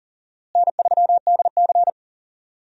Morse code, heard as a keyed tone, about 40 wpm, 700 Hz. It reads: N3DC
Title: USA Call Signs - 4 Characters - Repeated 2x - 40wpm